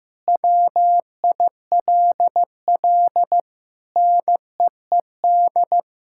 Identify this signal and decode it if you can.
Morse code, heard as a keyed tone, about 15 wpm, 700 Hz. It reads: WILL NEED
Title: Sentences from Top 1600 Words 15wpm